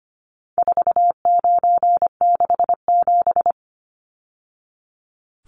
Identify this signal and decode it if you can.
Morse code, heard as a keyed tone, about 25 wpm, 700 Hz. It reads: 4967
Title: Numbers - 4-Digit 25wpm